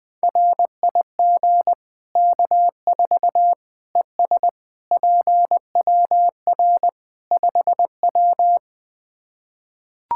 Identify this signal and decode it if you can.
Morse code, heard as a keyed tone, about 20 wpm, 700 Hz. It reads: RIG K4 ES PWR 5W
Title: K1USN SST QSO Practice 20wpm